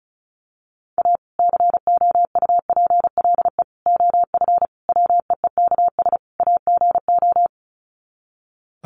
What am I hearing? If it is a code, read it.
Morse code, heard as a keyed tone, about 34 wpm, 700 Hz. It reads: A COUPLE OF WEEKS AGO
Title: Sets of 5 Words 35wpm